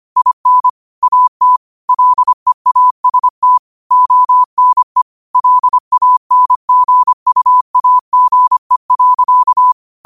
Morse code, heard as a keyed tone, about 25 wpm, 1000 Hz. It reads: IN AT LEAST ONE LANGUAGE.
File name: qotdfiles / 25wpm